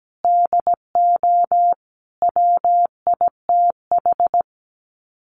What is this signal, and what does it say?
Morse code, heard as a keyed tone, about 17 wpm, 690 Hz. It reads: DO WITH
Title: Sets of 5 Words Encore 17wpm